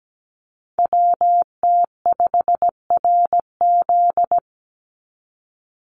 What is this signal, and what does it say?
Morse code, heard as a keyed tone, about 17 wpm, 700 Hz. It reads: WT5RZ
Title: Call Signs Repeated 3x - 17wpm